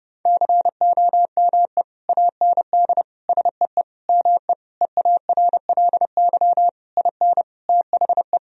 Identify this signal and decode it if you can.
Morse code, heard as a keyed tone, about 30 wpm, 700 Hz. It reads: COME AND SEE ME EARLY IN THE
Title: Sentences from Top 1000 Words - Review Part 2 - 30wpm